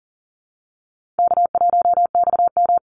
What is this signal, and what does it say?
Morse code, heard as a keyed tone, about 40 wpm, 700 Hz. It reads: K1XM
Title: Easy Call Signs Repeated 3x - 40wpm